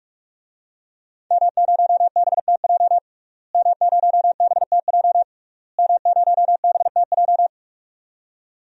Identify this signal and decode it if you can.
Morse code, heard as a keyed tone, about 46 wpm, 700 Hz. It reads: M0BTJ M0BTJ M0BTJ
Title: DXCC-100 Call Signs - 5 Characters - Repeated 3x - 45wpm